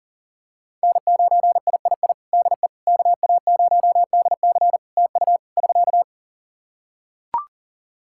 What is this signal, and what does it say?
Morse code, heard as a keyed tone, about 40 wpm, 700 Hz. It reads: N9III DE KA0DC TU <SK>
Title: Practice QSO - Exchange 4 - Ending - 40wpm